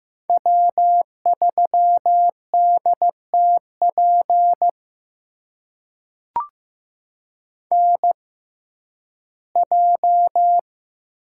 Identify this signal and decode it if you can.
Morse code, heard as a keyed tone, about 15 wpm, spaced as for 11 wpm, 700 Hz. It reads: W3DTP NJ
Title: USA Call Signs - 6x Character Spacing - 15wpm